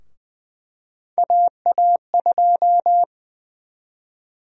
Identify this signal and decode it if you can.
Morse code, heard as a keyed tone, about 20 wpm, 700 Hz. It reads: AA2
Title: USA Call Sign Prefixes - 20wpm